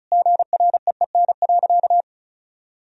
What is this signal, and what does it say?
Morse code, heard as a keyed tone, about 36 wpm, 700 Hz. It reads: GREEN.
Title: Sentences from Top 1000 Words - Review Part 2 - 35wpm